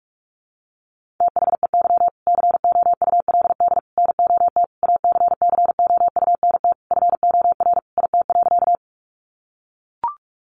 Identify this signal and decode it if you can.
Morse code, heard as a keyed tone, about 44 wpm, 700 Hz. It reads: THEY COULD NOT ACCOUNT FOR IT.
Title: Sentences from Top 900 Words 45wpm